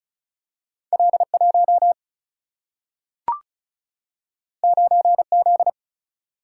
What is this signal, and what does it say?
Morse code, heard as a keyed tone, about 36 wpm, 700 Hz. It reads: L1 9Z
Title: Single-Letter Single-Number 35wpm